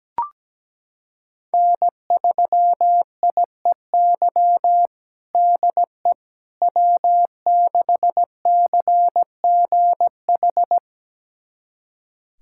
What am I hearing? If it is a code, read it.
Morse code, heard as a keyed tone, about 17 wpm, 700 Hz. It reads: N3IEY DE W6CGH
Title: USA Callsign DE Callsign - 5 Characters - 17wpm